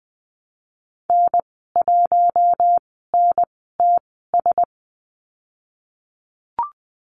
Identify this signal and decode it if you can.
Morse code, heard as a keyed tone, about 20 wpm, spaced as for 15 wpm, 700 Hz. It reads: N1NTS
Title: 062 Lesson 11g - Letter h - Call Signs 20wpm